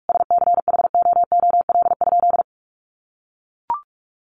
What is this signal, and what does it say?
Morse code, heard as a keyed tone, about 44 wpm, 700 Hz. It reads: SCHOOL?
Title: Sentences from Top 1400 Words 45wpm